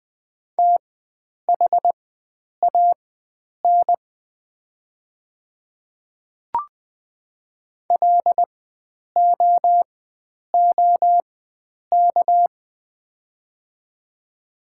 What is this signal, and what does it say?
Morse code, heard as a keyed tone, about 20 wpm, spaced as for 9 wpm, 700 Hz. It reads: THAN LOOK
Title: 4x Character Spacing - Top 100 Words - 20wpm